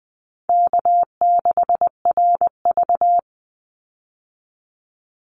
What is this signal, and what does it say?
Morse code, heard as a keyed tone, about 20 wpm, 700 Hz. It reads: K6RV
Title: Easy Call Signs Repeated 2x - 20wpm